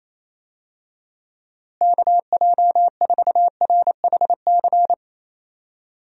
Morse code, heard as a keyed tone, about 28 wpm, 700 Hz. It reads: KJ4RHC
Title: USA Call Signs - 6 Characters - Repeated 2x - 28wpm